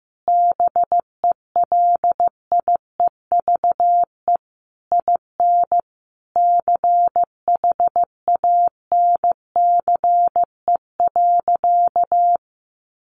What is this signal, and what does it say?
Morse code, heard as a keyed tone, about 15 wpm, 700 Hz. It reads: BELIEVE IN CHANCE.
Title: Sentences from Top 1000 Words 15wpm